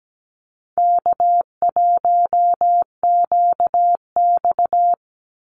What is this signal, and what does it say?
Morse code, heard as a keyed tone, about 17 wpm, 700 Hz. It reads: K1QX
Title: Easy Call Signs Repeated 3x - 17wpm